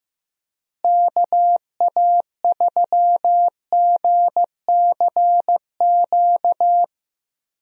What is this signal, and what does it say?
Morse code, heard as a keyed tone, about 15 wpm, 700 Hz. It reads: KA3GCQ
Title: USA Call Signs - 6 Characters - Repeated 3x - 15wpm